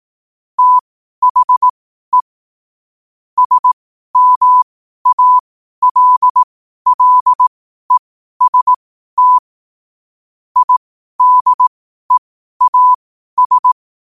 Morse code, heard as a keyed tone, about 18 wpm, spaced as for 12 wpm, 1000 Hz. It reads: THE SMALLEST IDEAS
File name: qotdfiles / 12wpm